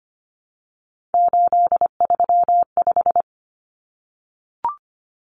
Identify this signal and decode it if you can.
Morse code, heard as a keyed tone, about 25 wpm, 700 Hz. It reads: 835
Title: Numbers - 3-Digit 25wpm